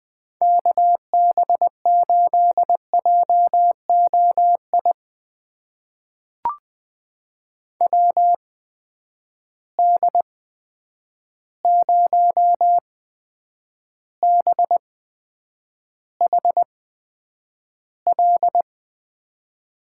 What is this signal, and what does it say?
Morse code, heard as a keyed tone, about 20 wpm, spaced as for 9 wpm, 700 Hz. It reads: KB8JOI WD0BHL
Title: USA Call Signs - 6 Characters - 8x Character Spacing - 20wpm